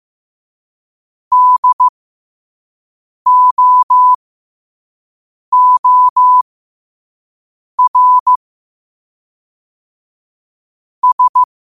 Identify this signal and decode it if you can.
Morse code, heard as a keyed tone, about 15 wpm, spaced as for 5 wpm, 1000 Hz. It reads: DOOR S
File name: qotdfiles / 5wpm